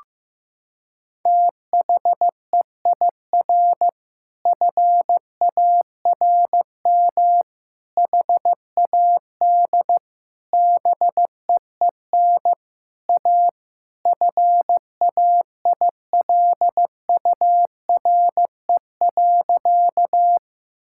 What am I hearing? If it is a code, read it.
Morse code, heard as a keyed tone, about 16 wpm, 700 Hz. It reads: THEIR FARM HAD BEEN A FAILURE.
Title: Sentences from Top 1000 Words - Review Part 6 - 15wpm